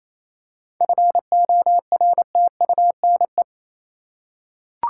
Morse code, heard as a keyed tone, about 28 wpm, 700 Hz. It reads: FORTUNE
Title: Seven Letter Words - 28wpm